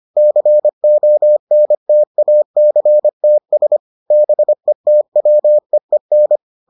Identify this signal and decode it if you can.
Morse code, heard as a keyed tone, about 25 wpm, 600 Hz. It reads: CONTACTS BETWEEN